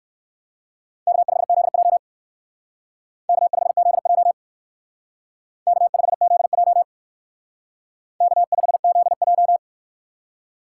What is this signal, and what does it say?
Morse code, heard as a keyed tone, about 56 wpm, 700 Hz. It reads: K5ZJ K5ZJ K5ZJ K5ZJ